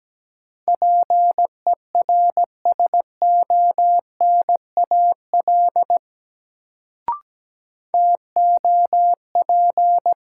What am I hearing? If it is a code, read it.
Morse code, heard as a keyed tone, about 17 wpm, 700 Hz. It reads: PERSONAL TOP
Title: Sentences from Top 700 Words - New Word Drill - 17wpm